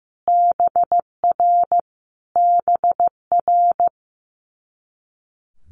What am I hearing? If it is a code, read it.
Morse code, heard as a keyed tone, about 15 wpm, 700 Hz. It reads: BR BR
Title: Multi-letter Phonemes Repeated 3x 15wpm